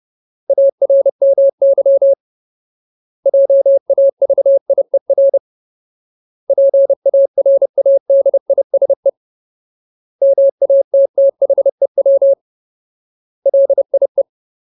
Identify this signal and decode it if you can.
Morse code, heard as a keyed tone, about 30 wpm, 550 Hz. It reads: ARMY JAVIER PARADISE MATTHEW LIE
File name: Ultimate Word List / 30WPM Ultimate